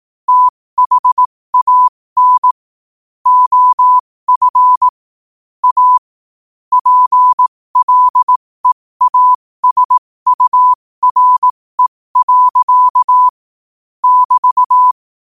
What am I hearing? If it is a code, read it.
Morse code, heard as a keyed tone, about 18 wpm, spaced as for 15 wpm, 1000 Hz. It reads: THAN OF A PLEASURE. <BT>
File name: qotdfiles / 15wpm